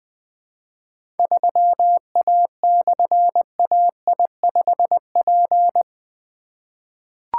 Morse code, heard as a keyed tone, about 20 wpm, 700 Hz. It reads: 3A/AI5P
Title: Difficult Call Signs 20wpm